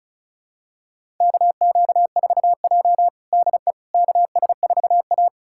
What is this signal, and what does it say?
Morse code, heard as a keyed tone, about 36 wpm, 700 Hz. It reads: KQ4J DE KS4A
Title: USA Callsign DE Callsign - 4 Characters - 35wpm